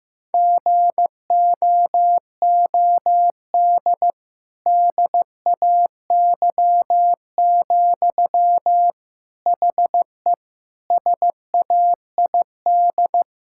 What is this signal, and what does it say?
Morse code, heard as a keyed tone, about 16 wpm, 700 Hz. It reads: GOOD DAY, HE SAID
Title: Sentences from Top 1600 Words 15wpm